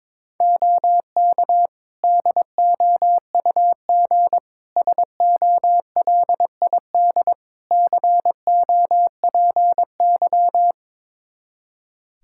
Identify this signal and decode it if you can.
Morse code, heard as a keyed tone, about 22 wpm, 700 Hz. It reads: OK DOUG SOLID COPY
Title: K1USN SST QSO Practice 22wpm